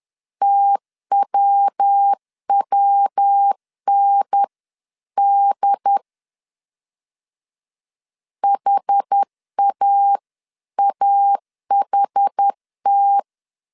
Morse code, heard as a keyed tone, about 10 wpm, 800 Hz. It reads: TWWND HAAHT
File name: CW / DET / 1 3 1500C S C 10wpm 3335